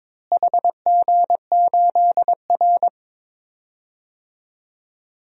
Morse code, heard as a keyed tone, about 22 wpm, 700 Hz. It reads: HG8R
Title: DXCC-100 Call Signs - 4 Characters - Repeated 3x - 22wpm